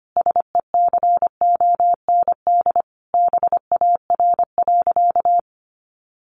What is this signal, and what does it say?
Morse code, heard as a keyed tone, about 25 wpm, 700 Hz. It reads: SECOND BAR.
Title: Easy Phrases 25wpm